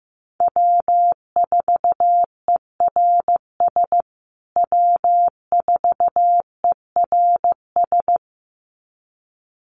Morse code, heard as a keyed tone, about 14 wpm, 700 Hz. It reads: W4ERS W4ERS
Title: USA Call Signs Repeated 2x 15wpm